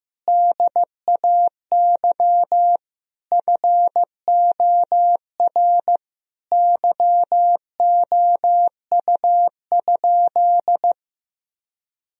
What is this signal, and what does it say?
Morse code, heard as a keyed tone, about 15 wpm, 700 Hz. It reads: DAY FOR YOU?
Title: Sentences from Top 1000 Words - Review Part 2 - 15wpm